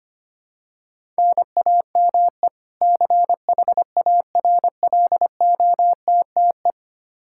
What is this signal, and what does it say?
Morse code, heard as a keyed tone, about 25 wpm, 700 Hz. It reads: NAME CHARLOTTE
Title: Practice QSO - Exchange 1 - Beginning - 25wpm